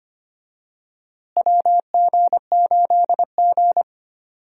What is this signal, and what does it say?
Morse code, heard as a keyed tone, about 25 wpm, 700 Hz. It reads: WG8G
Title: USA Call Signs - 4 Characters - Repeated 2x - 25wpm